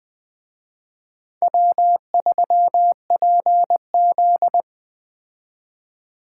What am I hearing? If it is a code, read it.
Morse code, heard as a keyed tone, about 21 wpm, 700 Hz. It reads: W3PZ
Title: Speed Racing - USA 1x2 Call Signs 17wpm